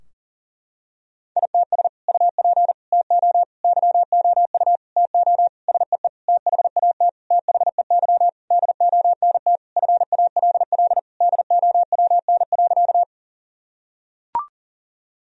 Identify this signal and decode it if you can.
Morse code, heard as a keyed tone, about 40 wpm, 700 Hz. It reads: ITS UP TO YOU TO SEE THAT THEY DONT FALL DOWN.
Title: Sentences from Top 1000 Words - Review Part 2 - 40wpm